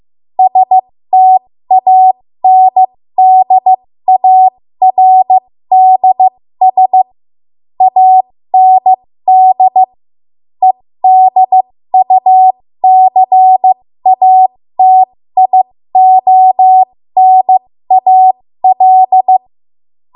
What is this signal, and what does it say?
Morse code, heard as a keyed tone, about 15 wpm, spaced as for 13 wpm, 750 Hz. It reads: STANDARDS AND EDUCATIONAL
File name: Morse / 13WPM